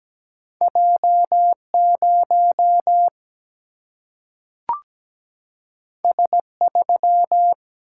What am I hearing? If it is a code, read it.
Morse code, heard as a keyed tone, about 17 wpm, 700 Hz. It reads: J 0 S 3
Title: Single-Letter Single-Number 17wpm